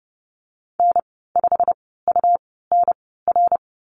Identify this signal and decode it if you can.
Morse code, heard as a keyed tone, about 30 wpm, spaced as for 17 wpm, 700 Hz. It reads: N5UNR
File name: Lesson 19g - Letter w - Call Signs 30wpm